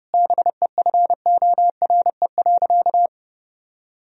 Morse code, heard as a keyed tone, about 32 wpm, 700 Hz. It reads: BEFORE.
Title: Sentences from Top 1000 Words - Review Part 1 - 30wpm